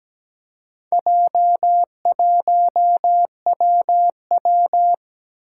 Morse code, heard as a keyed tone, about 17 wpm, 700 Hz. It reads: J1WW
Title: Four Challenging Letter-Number Combinations 17wpm